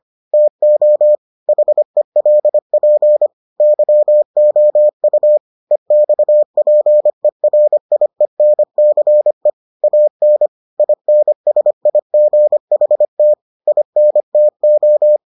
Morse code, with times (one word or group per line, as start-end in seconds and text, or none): 0.3-1.1 TO
1.5-3.3 HELP
3.6-5.4 YOU
5.7-9.5 EXPERIENCE
9.8-10.5 AN
10.8-13.3 INSIGHT
13.7-15.2 INTO